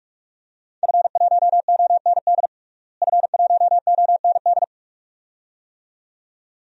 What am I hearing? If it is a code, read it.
Morse code, heard as a keyed tone, about 46 wpm, 700 Hz. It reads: F1OND F1OND
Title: DXCC-100 Call Signs - 5 Characters - Repeated 2x - 45wpm